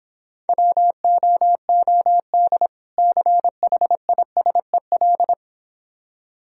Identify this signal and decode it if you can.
Morse code, heard as a keyed tone, about 26 wpm, 700 Hz. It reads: WOOD CHISEL